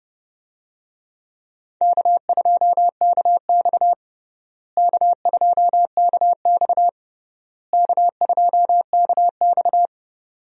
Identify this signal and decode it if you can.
Morse code, heard as a keyed tone, about 30 wpm, 700 Hz. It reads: K2KX K2KX K2KX
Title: USA Call Signs - 4 Characters - Repeated 3x - 30wpm